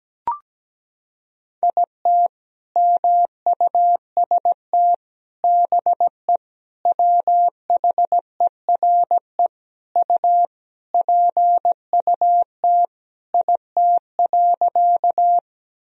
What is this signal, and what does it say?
Morse code, heard as a keyed tone, about 17 wpm, 700 Hz. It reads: IT MUST BE WHERE U PUT IT.
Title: Sentences from Top 200 Words and Abbreviations 17wpm